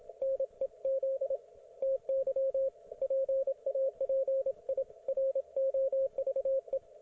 Morse code, heard as a keyed tone, about 27 wpm, 540 Hz, with radio noise. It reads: NEZ TY PAPIROVE